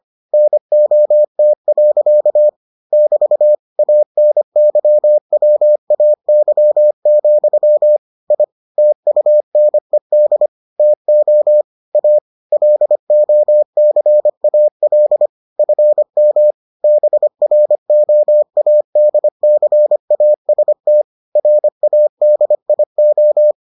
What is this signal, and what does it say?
Morse code, heard as a keyed tone, about 25 wpm, 600 Hz. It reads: NOT. <BT> ANYWAY, I TUNED TO A LOCAL FM BROADCAST RADIO